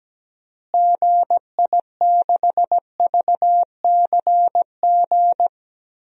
Morse code, heard as a keyed tone, about 17 wpm, 700 Hz. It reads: GI6VCG
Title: DXCC-100 Call Signs - 6 Characters - Repeated 2x - 17wpm